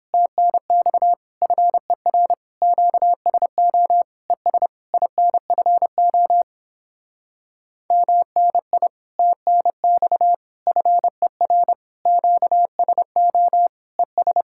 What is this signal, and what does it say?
Morse code, heard as a keyed tone, about 29 wpm, 700 Hz. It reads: TNX FER QSO ES INFO MNI TNX FER QSO ES